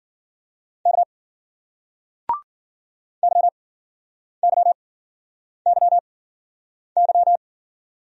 Morse code, heard as a keyed tone, about 50 wpm, spaced as for 9 wpm, 700 Hz. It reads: K YYYY